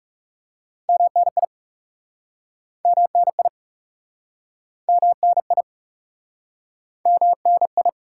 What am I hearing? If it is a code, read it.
Morse code, heard as a keyed tone, about 40 wpm, 700 Hz. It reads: MNI MNI MNI MNI